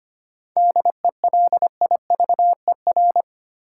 Morse code, heard as a keyed tone, about 25 wpm, 700 Hz. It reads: DELIVER
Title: Seven Letter Words - 25wpm